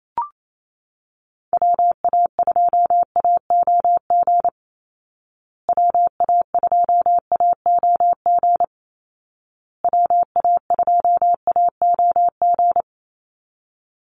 Morse code, heard as a keyed tone, about 28 wpm, 700 Hz. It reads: WA2AOG WA2AOG WA2AOG
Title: Call Signs Repeated 2x - 28wpm